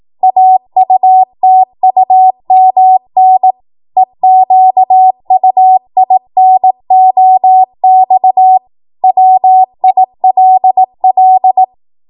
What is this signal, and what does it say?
Morse code, heard as a keyed tone, about 18 wpm, 750 Hz. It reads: AUTUMN EQUINOX WILL
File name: Morse / 18WPM